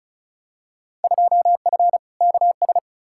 Morse code, heard as a keyed tone, about 36 wpm, 700 Hz. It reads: 2F KS
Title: Field Day Exchanges Encore - 35wpm